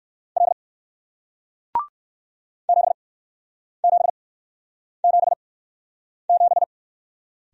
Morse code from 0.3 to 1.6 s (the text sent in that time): L